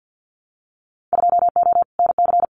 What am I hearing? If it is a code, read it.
Morse code, heard as a keyed tone, about 50 wpm, 700 Hz. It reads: 2O NC
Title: Winter Field Day Exchanges 50wpm